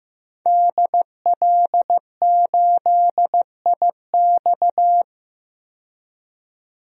Morse code, heard as a keyed tone, about 15 wpm, 700 Hz. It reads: DL8IX
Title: DXCC-100 Call Signs - 5 Characters - 15wpm